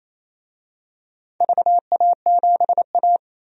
Morse code, heard as a keyed tone, about 27 wpm, 700 Hz. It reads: VA7A
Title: DXCC-100 Call Signs - 4 Characters - 28wpm